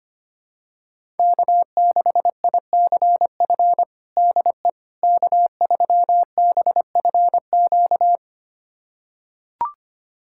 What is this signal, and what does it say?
Morse code, heard as a keyed tone, about 25 wpm, 700 Hz. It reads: K6ICF DE K3BFQ
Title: USA Callsign DE Callsign - 5 Characters - 25wpm